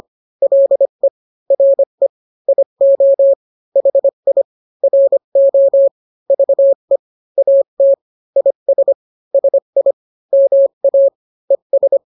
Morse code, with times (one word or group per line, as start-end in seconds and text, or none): 0.4-1.1 LE
1.5-2.1 RE
2.5-3.3 IO
3.8-4.4 HI
4.8-5.9 RO
6.3-7.0 VE
7.4-7.9 AT
8.4-8.9 IS
9.3-9.9 SI
10.3-11.1 MA
11.5-12.0 ES